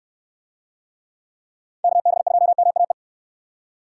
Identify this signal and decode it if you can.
Morse code, heard as a keyed tone, about 69 wpm, 700 Hz. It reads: KB2MAE